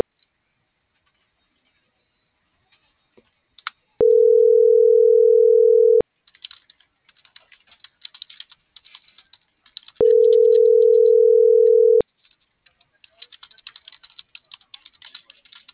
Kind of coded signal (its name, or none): telephone call progress tone